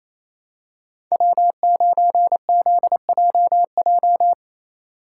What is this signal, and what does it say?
Morse code, heard as a keyed tone, about 29 wpm, 700 Hz. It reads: W9ZJJ
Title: USA Call Signs - 5 Characters - 4x Character Spacing - 28wpm